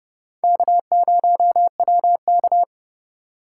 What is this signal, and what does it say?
Morse code, heard as a keyed tone, about 31 wpm, 700 Hz. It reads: K0WK